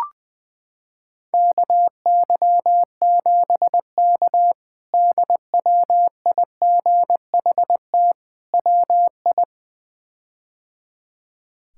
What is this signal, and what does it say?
Morse code, heard as a keyed tone, about 20 wpm, 700 Hz. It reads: KY7K DWIGHT WI
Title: K1USN SST QSO Practice - 20wpm